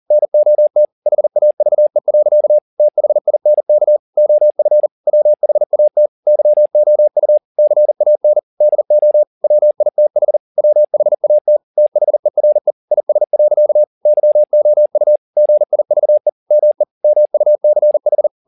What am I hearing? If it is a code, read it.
Morse code, heard as a keyed tone, about 40 wpm, 600 Hz. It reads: NOT HAVE. THINK OF WHAT YOU CAN DO WITH WHAT THERE IS. YOU GIVE ME MUCH
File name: the old man and the sea 2 40wpm